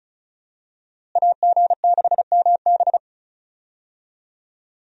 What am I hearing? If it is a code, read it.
Morse code, heard as a keyed tone, about 36 wpm, 700 Hz. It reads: AG6MB
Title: USA Call Signs - 5 Characters - Repeated 2x - 35wpm